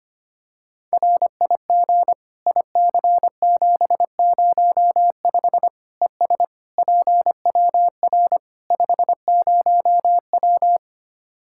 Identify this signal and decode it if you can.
Morse code, heard as a keyed tone, about 25 wpm, 700 Hz. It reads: RIG IC705 ES PWR 50W
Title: Practice QSO - Exchange 2 - Info - 25wpm